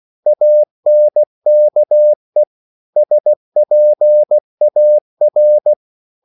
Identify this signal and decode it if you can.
Morse code, heard as a keyed tone, about 16 wpm, 600 Hz. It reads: ANKE SPAR
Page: Korte danske ord 16wpm